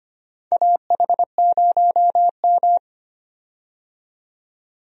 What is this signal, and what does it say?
Morse code, heard as a keyed tone, about 25 wpm, 700 Hz. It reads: AH0M
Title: USA Call Signs - 4 Characters - 25wpm